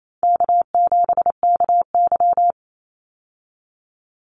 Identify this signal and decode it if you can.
Morse code, heard as a keyed tone, about 28 wpm, 700 Hz. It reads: K7KY
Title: Call Signs Repeated 3x - 28wpm